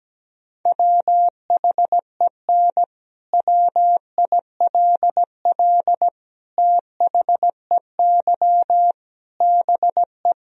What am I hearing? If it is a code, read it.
Morse code, heard as a keyed tone, about 17 wpm, 700 Hz. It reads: WHEN WILL THEY BE
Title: Sentences from Top 1000 Words - Review Part 3 - 17wpm